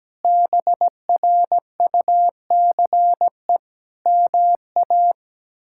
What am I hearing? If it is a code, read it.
Morse code, heard as a keyed tone, about 17 wpm, 700 Hz. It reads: BRUCE MA
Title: K1USN SST QSO Practice - 17wpm